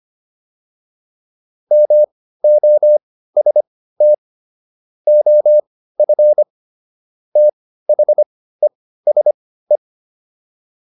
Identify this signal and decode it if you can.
Morse code, heard as a keyed tone, about 25 wpm, spaced as for 15 wpm, 600 Hz. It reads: MOST OF THESE